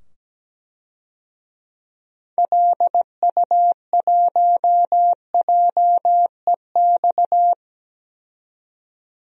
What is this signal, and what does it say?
Morse code, heard as a keyed tone, about 17 wpm, 700 Hz. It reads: LU1JEX
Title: DXCC-100 Call Signs - 6 Characters - 17wpm